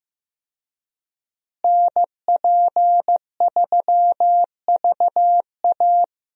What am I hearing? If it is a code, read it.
Morse code, heard as a keyed tone, about 15 wpm, 700 Hz. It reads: NP3VA
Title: USA Call Signs - 5 Characters - Repeated 2x - 15wpm